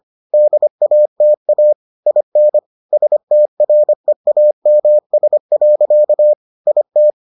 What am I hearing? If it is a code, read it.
Morse code, heard as a keyed tone, about 25 wpm, 600 Hz. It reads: DATA IN STREAMS. IT